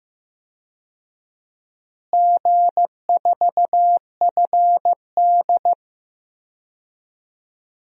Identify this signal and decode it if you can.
Morse code, heard as a keyed tone, about 15 wpm, 700 Hz. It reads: G4FD
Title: DXCC-100 Call Signs - 4 Characters - Repeated 3x - 15wpm